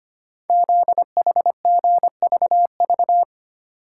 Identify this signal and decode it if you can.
Morse code, heard as a keyed tone, about 25 wpm, 700 Hz. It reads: ZHGVV
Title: Code Groups - 5 Letters - 25wpm